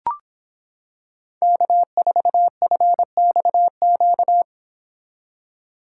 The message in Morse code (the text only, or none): K4FXQ